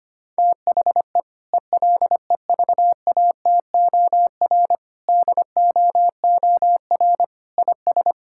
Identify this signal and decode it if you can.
Morse code, heard as a keyed tone, about 25 wpm, 700 Hz. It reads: THE ELEVATOR DOOR IS
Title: Sentences from Top 1600 Words 25wpm